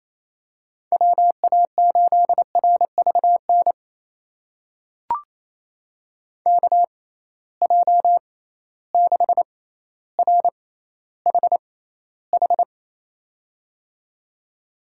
Morse code, heard as a keyed tone, about 28 wpm, spaced as for 14 wpm, 700 Hz. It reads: WA8RVN KJ6RHH
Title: USA Call Signs - 6 Characters - 6x Character Spacing - 28wpm